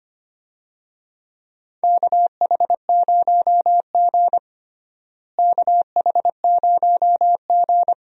Morse code, heard as a keyed tone, about 25 wpm, 700 Hz. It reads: KH0G KH0G
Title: USA Call Signs - 4 Characters - Repeated 3x - 25wpm